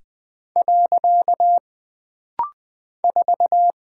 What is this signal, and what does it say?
Morse code, heard as a keyed tone, about 20 wpm, 700 Hz. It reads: . 4
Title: Single Character (Letter:Number:Punctuation:Prosign) - Rapid-Fire - 20wpm